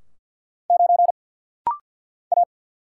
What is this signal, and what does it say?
Morse code, heard as a keyed tone, about 49 wpm, 700 Hz. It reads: 9 A
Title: Single Letter-Number - Rapid-Fire 50wpm